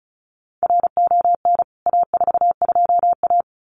Morse code, heard as a keyed tone, about 34 wpm, 700 Hz. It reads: RON A42A
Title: CWT Exchange 35wpm